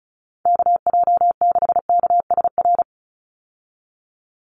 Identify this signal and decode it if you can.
Morse code, heard as a keyed tone, about 34 wpm, 700 Hz. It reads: KJ6KSR
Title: USA Call Signs 35wpm